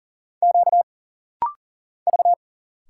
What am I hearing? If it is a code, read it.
Morse code, heard as a keyed tone, about 40 wpm, 700 Hz. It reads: Q V
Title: Single Letters - Rapid-Fire 40wpm